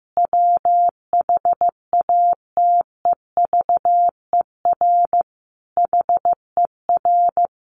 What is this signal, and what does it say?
Morse code, heard as a keyed tone, about 15 wpm, 700 Hz. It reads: WHATEVER HER
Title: Sentences from Top 1100 Words 15wpm